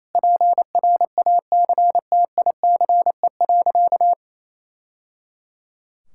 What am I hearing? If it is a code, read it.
Morse code, heard as a keyed tone, about 28 wpm, 700 Hz. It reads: PRACTICE.
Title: Sentences from Top 1000 Words - Review Part 6 - 28wpm